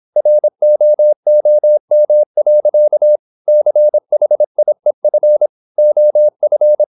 Morse code, heard as a keyed tone, about 26 wpm, 600 Hz. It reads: ROOM. CHIEF OF
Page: Never 26wpm | CW med Gnister